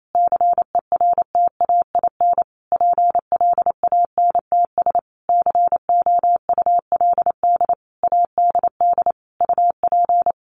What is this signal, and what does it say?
Morse code, heard as a keyed tone, about 28 wpm, 700 Hz. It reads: CERTAIN PLANTS COULD ADD UP
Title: A Sound Of Thunder 28wpm